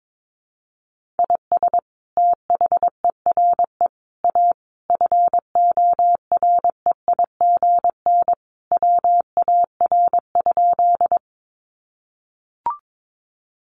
Morse code, heard as a keyed tone, about 22 wpm, 700 Hz. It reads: IS THERE A FOREIGN WAR?
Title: Sentences from Top 1400 Words 22wpm